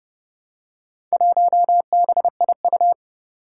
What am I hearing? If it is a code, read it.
Morse code, heard as a keyed tone, about 30 wpm, 700 Hz. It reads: 1BIU
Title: Four Challenging Letter-Number Combinations 30wpm